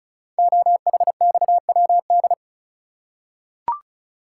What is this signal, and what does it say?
Morse code, heard as a keyed tone, about 36 wpm, 700 Hz. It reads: OHXWD
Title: Code Groups - 5 Letters - 35wpm